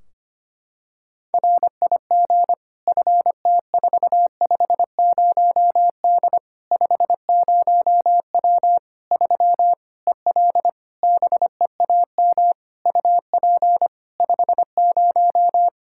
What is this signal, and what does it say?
Morse code, heard as a keyed tone, about 25 wpm, 700 Hz. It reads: RIG FT450D 50W 3 EL BEAM UP 50
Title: K1USN SST QSO Practice 25wpm